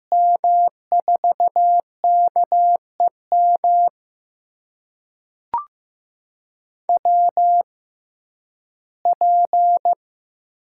Morse code, heard as a keyed tone, about 15 wpm, spaced as for 11 wpm, 700 Hz. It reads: M4KEM WP
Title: USA Call Signs - 6 Characters - 6x Character Spacing - 15wpm